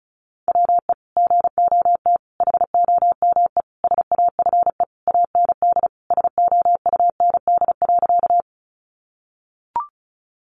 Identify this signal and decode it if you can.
Morse code, heard as a keyed tone, about 34 wpm, 700 Hz. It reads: WE GOT HOME SAFE AND SOUND.
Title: Sentences from Top 900 Words 35wpm